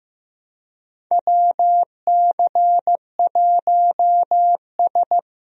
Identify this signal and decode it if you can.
Morse code, heard as a keyed tone, about 16 wpm, 700 Hz. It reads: WC1S
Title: USA Call Signs - 4 Characters - Repeated 3x - 15wpm